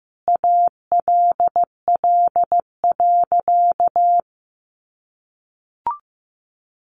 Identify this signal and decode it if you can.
Morse code, heard as a keyed tone, about 15 wpm, 700 Hz. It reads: ALL.
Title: Sentences from Top 1200 Words 15wpm